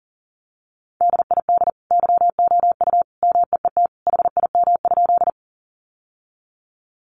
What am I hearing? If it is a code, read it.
Morse code, heard as a keyed tone, about 40 wpm, 700 Hz. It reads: DID YOU MEET HIM?
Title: Sentences from Top 1100 Words 40wpm